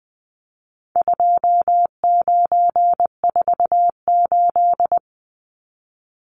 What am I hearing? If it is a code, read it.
Morse code, heard as a keyed tone, about 20 wpm, 700 Hz. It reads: 2948
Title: Numbers - 4-Digit 20wpm